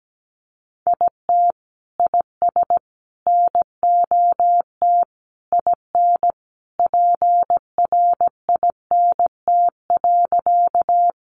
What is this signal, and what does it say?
Morse code, heard as a keyed tone, about 17 wpm, 700 Hz. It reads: IT IS NOT IN PRINT.
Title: Sentences from Top 1400 Words 17wpm